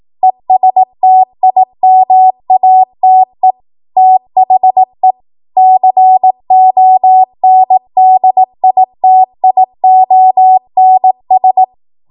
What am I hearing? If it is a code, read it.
Morse code, heard as a keyed tone, about 18 wpm, 750 Hz. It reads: ESTIMATE THE CONDITIONS
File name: Archive / 18 WPM / 18WPM